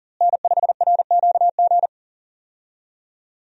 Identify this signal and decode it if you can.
Morse code, heard as a keyed tone, about 40 wpm, 690 Hz. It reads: N5RQG
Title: USA Call Signs - 5 Characters - Repeated 3x - 40wpm